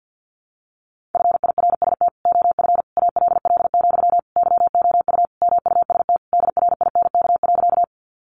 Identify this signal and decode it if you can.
Morse code, heard as a keyed tone, about 50 wpm, 700 Hz. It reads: FIRST OF ALL, YOU MUST DRINK.
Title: Sentences from Top 1500 Words 50wpm